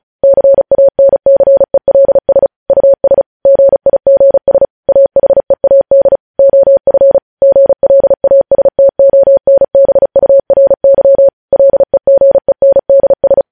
Morse code, heard as a keyed tone, about 36 wpm, 550 Hz. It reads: CANCELS US GIGS AHEAD OF GLASTONBURY LEGENDS
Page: news headlines in morse code at 35 wpm